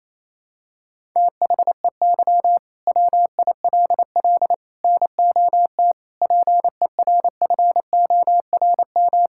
Morse code, heard as a keyed tone, about 28 wpm, 700 Hz. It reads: THEY WILL NOT PERFORM
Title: Sentences from Top 1000 Words - Review Part 5 - 28wpm